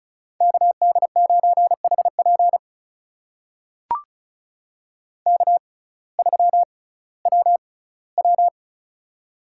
Morse code, heard as a keyed tone, about 36 wpm, spaced as for 19 wpm, 700 Hz. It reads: KD9HP K3WW
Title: USA Call Signs - 6x Character Spacing - 35wpm